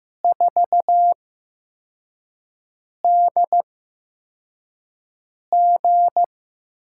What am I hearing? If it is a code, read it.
Morse code, heard as a keyed tone, about 15 wpm, spaced as for 4 wpm, 700 Hz. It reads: 4DG
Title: USA Call Signs - 4 Characters - 8x Character Spacing - 15wpm